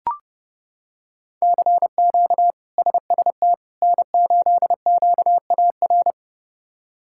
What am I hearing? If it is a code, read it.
Morse code, heard as a keyed tone, about 32 wpm, 700 Hz. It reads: CQ SST N8QAR
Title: K1USN SST QSO Practice - 30wpm